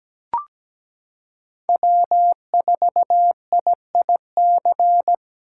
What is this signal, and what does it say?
Morse code, heard as a keyed tone, about 17 wpm, 700 Hz. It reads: W4IIC
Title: K1USN SST QSO Practice - 17wpm